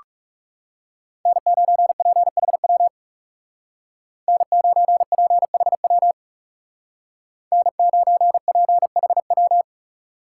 Morse code, heard as a keyed tone, about 37 wpm, 700 Hz. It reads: N9PHW N9PHW N9PHW
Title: Speed Racing - USA 1x3 Call Signs 30wpm